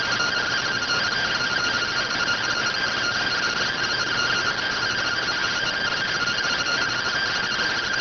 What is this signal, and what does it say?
Two-tone teleprinter signal, about 170 Hz shift.